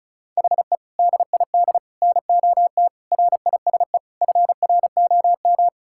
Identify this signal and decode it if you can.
Morse code, heard as a keyed tone, about 34 wpm, 700 Hz. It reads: HE DID NOT RISE FROM
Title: Sentences from Top 1000 Words - Review Part 6 - 35wpm